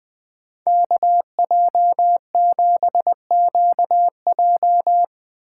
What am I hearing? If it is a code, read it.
Morse code, heard as a keyed tone, about 21 wpm, 700 Hz. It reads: KJ7QJ
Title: Speed Racing - USA Call Signs 20wpm